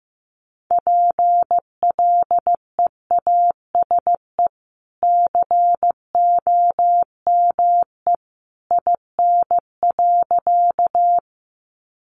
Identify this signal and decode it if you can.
Morse code, heard as a keyed tone, about 15 wpm, 700 Hz. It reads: PLEASE COME IN.
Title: Sentences from Top 1500 Words 15wpm